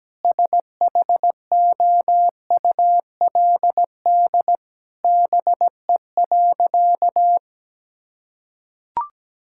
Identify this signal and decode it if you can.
Morse code, heard as a keyed tone, about 17 wpm, 700 Hz. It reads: SHOULD BE.
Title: Sentences from Top 1000 Words - Review Part 6 - 17wpm